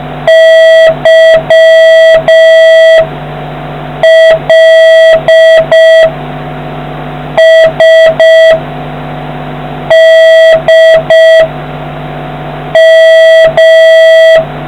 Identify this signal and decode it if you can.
Morse code, heard as a keyed tone, about 5 wpm, 1950 Hz, with noise. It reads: Y L S D M